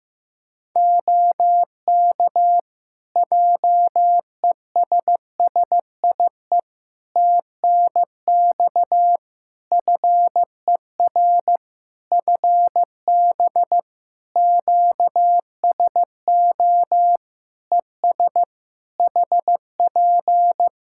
Morse code, heard as a keyed tone, about 15 wpm, 700 Hz. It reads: OK JESSIE TNX FER FB QSO ES HP
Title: Practice QSO - Exchange 4 - Ending - 15wpm